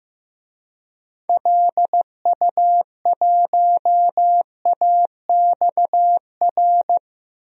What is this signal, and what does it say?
Morse code, heard as a keyed tone, about 15 wpm, 700 Hz. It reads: LU1AXR
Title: DXCC-100 Call Signs - 6 Characters - 15wpm